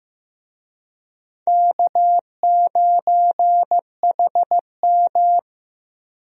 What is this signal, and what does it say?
Morse code, heard as a keyed tone, about 16 wpm, 700 Hz. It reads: K9HM
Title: USA Call Signs - 4 Characters - 15wpm